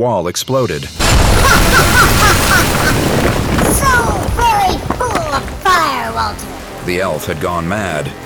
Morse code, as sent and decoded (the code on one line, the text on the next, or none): none